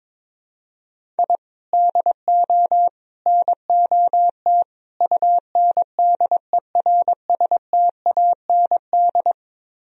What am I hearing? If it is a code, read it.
Morse code, heard as a keyed tone, about 23 wpm, 700 Hz. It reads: I DO NOT UNDERSTAND
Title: Speed-Racing - Sets of 4 Words 22wpm